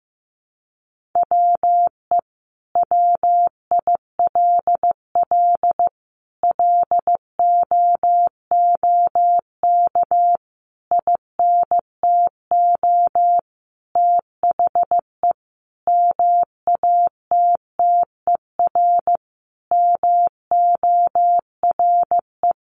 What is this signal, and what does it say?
Morse code, heard as a keyed tone, about 15 wpm, 700 Hz. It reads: WE WILL LOOK INTO THE MATTER MORE
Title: Sentences from Top 1400 Words 15wpm